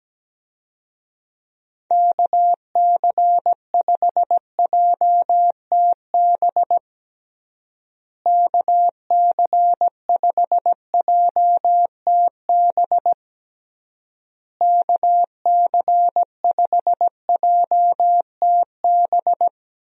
Morse code, heard as a keyed tone, about 17 wpm, 700 Hz. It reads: KC5JTB KC5JTB KC5JTB
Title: USA Call Signs - 6 Characters - Repeated 3x - 17wpm